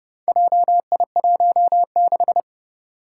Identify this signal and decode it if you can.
Morse code, heard as a keyed tone, about 30 wpm, 700 Hz. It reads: JI16
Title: Four Challenging Letter-Number Combinations 30wpm